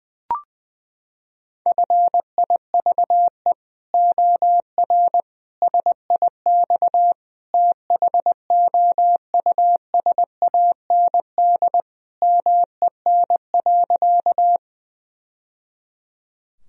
Morse code, heard as a keyed tone, about 19 wpm, 700 Hz. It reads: FIVE OR SIX THOUSAND MEN.
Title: Sentences from Top 1000 Words - Review Part 3 - 20wpm